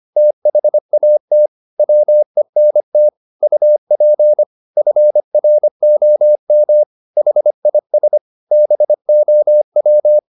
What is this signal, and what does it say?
Morse code, heard as a keyed tone, about 25 wpm, 600 Hz. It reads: THAT WENT UP FROM HIS BOW